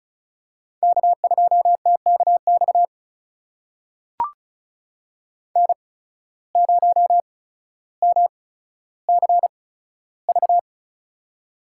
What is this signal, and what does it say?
Morse code, heard as a keyed tone, about 36 wpm, spaced as for 15 wpm, 700 Hz. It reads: K2TKX N0MCV
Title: USA Call Signs - 5 Characters - 8x Character Spacing - 35wpm